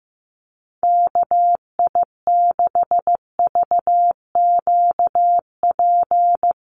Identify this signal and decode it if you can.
Morse code, heard as a keyed tone, about 15 wpm, 700 Hz. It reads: KI6VQP
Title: USA Call Signs Repeated 3x 15wpm